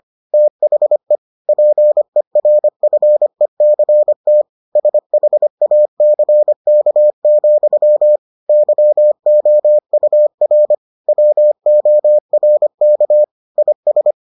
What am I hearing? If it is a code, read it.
Morse code, heard as a keyed tone, about 25 wpm, 600 Hz. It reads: THE PERFECT SHACK, YOUR WORK IS